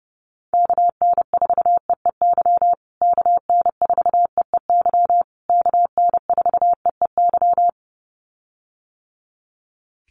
Morse code, heard as a keyed tone, about 30 wpm, 700 Hz. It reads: KN4EEY KN4EEY KN4EEY
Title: USA Call Signs Repeated 3x 30wpm